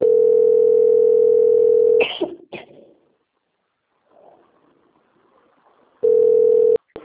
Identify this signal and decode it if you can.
Ringback tone (North America), 440 + 480 Hz.
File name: out- - - - - .30984